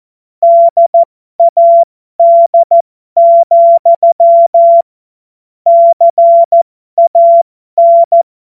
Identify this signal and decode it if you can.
Morse code, heard as a keyed tone, about 14 wpm, spaced as for 12 wpm, 680 Hz. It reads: DAD, CAN
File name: Morsekurs / Witze